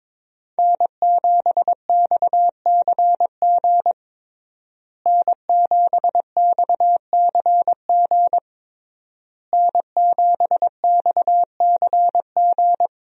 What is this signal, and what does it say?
Morse code, heard as a keyed tone, about 22 wpm, 700 Hz. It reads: N7XCG N7XCG N7XCG
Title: USA Call Signs - 5 Characters - Repeated 3x - 22wpm